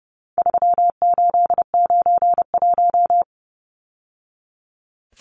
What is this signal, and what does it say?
Morse code, heard as a keyed tone, about 30 wpm, 700 Hz. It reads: 3891
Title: Numbers - 4-Digit 30wpm